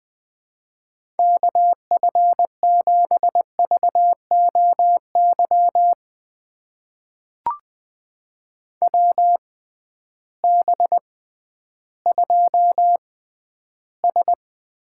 Morse code, heard as a keyed tone, about 20 wpm, spaced as for 12 wpm, 700 Hz. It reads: KF7VOY WB2S
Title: USA Call Signs - 6 Characters - 6x Character Spacing - 20wpm